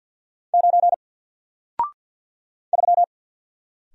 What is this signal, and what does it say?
Morse code, heard as a keyed tone, about 50 wpm, 700 Hz. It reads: 9 3
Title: Single Letter-Number Alt 50wpm